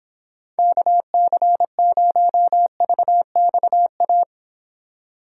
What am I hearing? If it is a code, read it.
Morse code, heard as a keyed tone, about 30 wpm, 700 Hz. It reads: KC0VXA